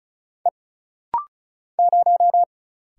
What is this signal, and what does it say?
Morse code, heard as a keyed tone, about 34 wpm, 700 Hz. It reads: E 0
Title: Single Letter-Number - Rapid-Fire 35wpm